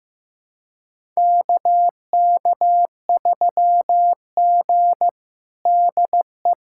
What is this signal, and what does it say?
Morse code, heard as a keyed tone, about 16 wpm, 700 Hz. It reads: KK3G DE
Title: USA Callsign DE Callsign - 4 Characters - 15wpm